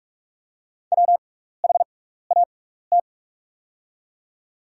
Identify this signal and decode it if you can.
Morse code, heard as a keyed tone, about 44 wpm, spaced as for 15 wpm, 690 Hz. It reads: WHAT
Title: 6x Character Spacing - Top 500 Words - 45wpm